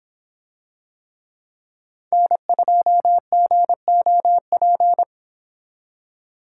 Morse code, heard as a keyed tone, about 24 wpm, 700 Hz. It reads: N2GOP